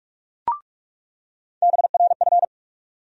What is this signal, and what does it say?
Morse code, heard as a keyed tone, about 46 wpm, 700 Hz. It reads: BRF
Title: USA Call Sign Postfixes - 3 Letters 45wpm